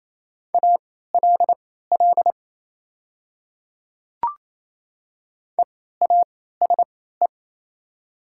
Morse code, heard as a keyed tone, about 28 wpm, spaced as for 16 wpm, 700 Hz. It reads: ALL EASE
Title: 3x Character Spacing - Top 300 Words - 28wpm